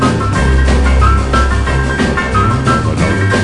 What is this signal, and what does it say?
There is no code.